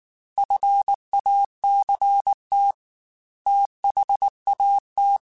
Morse code, heard as a keyed tone, about 19 wpm, 800 Hz. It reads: FACT THAT
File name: Technical Subjects / Technical Subjects-mp3s / Reactance @18